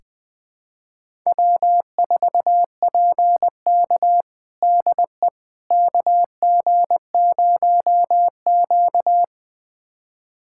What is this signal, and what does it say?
Morse code, heard as a keyed tone, about 21 wpm, 700 Hz. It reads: W4PK DE KG0Q
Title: USA Callsign DE Callsign - 4 Characters - 20wpm